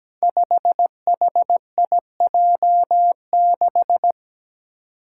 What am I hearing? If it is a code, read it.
Morse code, heard as a keyed tone, about 17 wpm, 700 Hz. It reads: 5HIJ6
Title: Five Challenging Letter-Number Combinations 17wpm